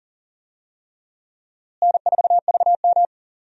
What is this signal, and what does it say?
Morse code, heard as a keyed tone, about 40 wpm, 700 Hz. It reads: N4VM
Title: USA Call Signs - 4 Characters - Repeated 3x - 40wpm